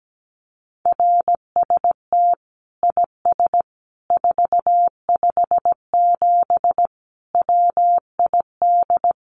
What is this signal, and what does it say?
Morse code, heard as a keyed tone, about 17 wpm, 700 Hz. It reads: RST IS 457 WID
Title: Common QSO Phrases 17wpm